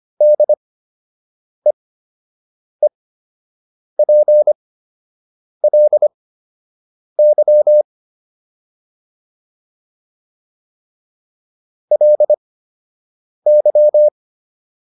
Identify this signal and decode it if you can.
Morse code, heard as a keyed tone, about 25 wpm, spaced as for 7 wpm, 600 Hz. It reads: DEEPLY LY
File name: LY6 07